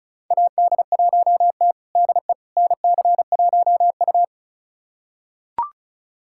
Morse code, heard as a keyed tone, about 36 wpm, 700 Hz. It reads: AD1T DE NC1U
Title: USA Callsign DE Callsign - 4 Characters - 35wpm